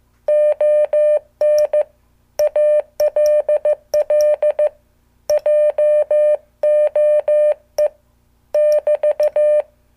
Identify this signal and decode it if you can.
Morse code, heard as a keyed tone, about 15 wpm, 600 Hz. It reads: ON ALL JOE <BT>